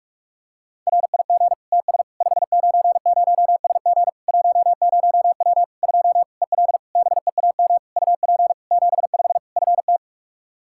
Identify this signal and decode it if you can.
Morse code, heard as a keyed tone, about 46 wpm, 700 Hz. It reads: RIG TS 590SG 10W 2 EL BEAM UP 75 FT
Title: Practice QSO - Exchange 2 - Info - 45wpm